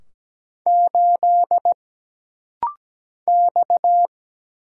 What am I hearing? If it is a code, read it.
Morse code, heard as a keyed tone, about 17 wpm, 700 Hz. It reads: 8 X
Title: Single Letter-Number - Rapid-Fire 17wpm